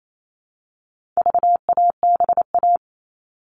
Morse code, heard as a keyed tone, about 28 wpm, 700 Hz. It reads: VABA
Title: All Cut Numbers - 4-Digit 28wpm